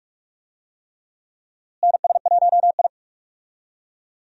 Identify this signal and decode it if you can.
Morse code, heard as a keyed tone, about 46 wpm, 700 Hz. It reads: NS1I
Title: USA Call Signs - 4 Characters - Repeated 3x - 45wpm